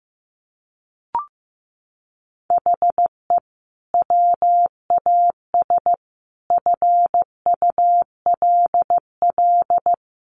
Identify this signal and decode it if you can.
Morse code, heard as a keyed tone, about 15 wpm, 700 Hz. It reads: HE WAS FULL
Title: Sentences from Top 1100 Words 15wpm